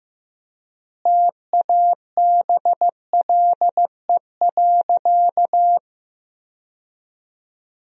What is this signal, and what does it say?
Morse code, heard as a keyed tone, about 15 wpm, 700 Hz. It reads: TABLE.
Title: 4x Word-Spacing - Sentences from Top 400 Words - 15wpm